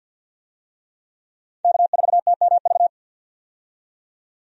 Morse code, heard as a keyed tone, about 52 wpm, 700 Hz. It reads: K4TMV
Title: USA Call Signs - 5 Characters - 50wpm